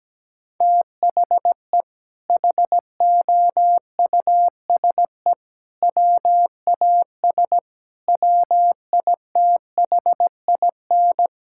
Morse code, heard as a keyed tone, about 17 wpm, 700 Hz. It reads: THE HOUSE WAS WITHIN
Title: Sentences from Top 1000 Words - Review Part 6 - 17wpm